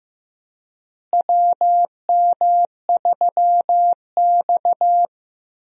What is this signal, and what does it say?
Morse code, heard as a keyed tone, about 15 wpm, 700 Hz. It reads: WM3X
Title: USA Call Signs - 4 Characters - Repeated 3x - 15wpm